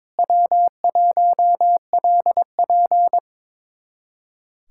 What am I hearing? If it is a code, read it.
Morse code, heard as a keyed tone, about 21 wpm, 700 Hz. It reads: W1LP
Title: USA Callsign DE Callsign - 4 Characters - 22wpm